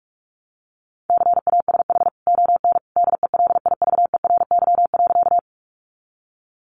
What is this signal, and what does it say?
Morse code, heard as a keyed tone, about 44 wpm, 700 Hz. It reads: CASH ON DELIVERY.
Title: Sentences from Top 1500 Words 45wpm